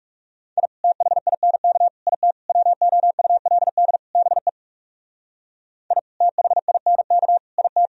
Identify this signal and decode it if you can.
Morse code, heard as a keyed tone, about 36 wpm, 700 Hz. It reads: I THINK IT WOULD BE I THINK IT
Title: Speed-Racing - Sets of 5 Words 30wpm